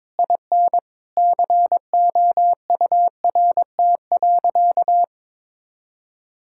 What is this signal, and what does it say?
Morse code, heard as a keyed tone, about 22 wpm, 700 Hz. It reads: IN COURT.
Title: Sentences from Top 1600 Words 22wpm